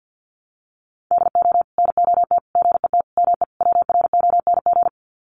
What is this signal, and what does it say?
Morse code, heard as a keyed tone, about 50 wpm, 700 Hz. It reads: DO NOT GET ME WRONG
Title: Sets of 5 Words Encore 50wpm